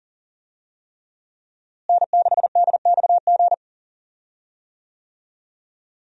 Morse code, heard as a keyed tone, about 37 wpm, 700 Hz. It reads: N6DXG
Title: USA Call Signs - 5 Characters - Repeated 3x - 40wpm